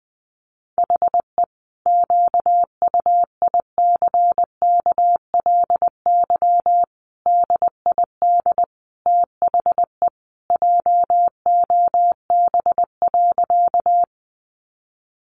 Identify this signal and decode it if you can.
Morse code, heard as a keyed tone, about 20 wpm, 700 Hz. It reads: HE QUICKLY DID THE JOB.
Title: Sentences from Top 700 Words 20wpm